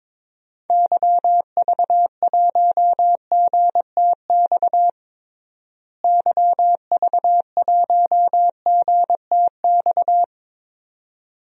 Morse code, heard as a keyed tone, about 22 wpm, 700 Hz. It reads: YV1GTX YV1GTX
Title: DXCC-100 Call Signs - 6 Characters - Repeated 2x - 22wpm